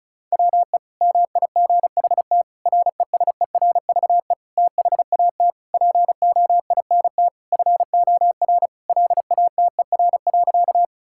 Morse code, heard as a keyed tone, about 36 wpm, 700 Hz. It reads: WE MIGHT RESERVE THAT POINT FOR LATER.
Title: Sentences from Top 1000 Words - Review Part 6 - 35wpm